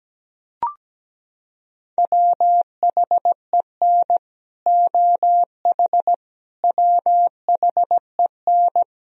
Morse code, heard as a keyed tone, about 17 wpm, 700 Hz. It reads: WHEN OH WHEN
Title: Sentences from Top 1000 Words - Review Part 2 - 17wpm